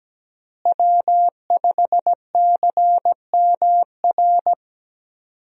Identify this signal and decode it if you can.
Morse code, heard as a keyed tone, about 17 wpm, 700 Hz. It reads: W5CMR
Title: USA Call Signs - 5 Characters - Repeated 3x - 17wpm